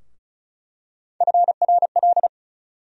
Morse code, heard as a keyed tone, about 36 wpm, 700 Hz. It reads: FRL
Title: USA Call Sign Postfixes - 3 Letters 35wpm